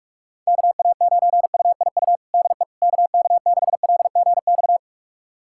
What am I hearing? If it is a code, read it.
Morse code, heard as a keyed tone, about 42 wpm, 700 Hz. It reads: KA9UIU DE KK6LGX
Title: USA Callsign DE Callsign - 6 Characters - 45wpm